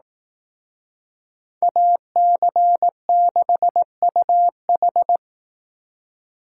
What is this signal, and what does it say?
Morse code, heard as a keyed tone, about 18 wpm, 700 Hz. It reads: AC6UH
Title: Speed Racing - USA Call Signs 15wpm